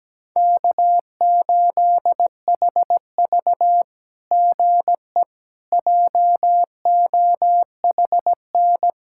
Morse code, heard as a keyed tone, about 17 wpm, 700 Hz. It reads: K8HV GE JOHN
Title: SOTA QSO Practice 17wpm